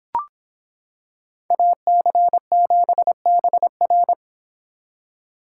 Morse code, heard as a keyed tone, about 26 wpm, 700 Hz. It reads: AC7BR